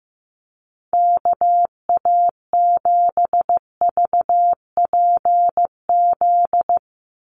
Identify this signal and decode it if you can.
Morse code, heard as a keyed tone, about 15 wpm, 700 Hz. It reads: KA7VPZ
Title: USA Call Signs 15wpm